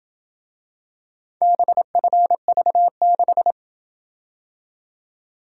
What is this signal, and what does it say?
Morse code, heard as a keyed tone, about 27 wpm, 700 Hz. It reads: BFV6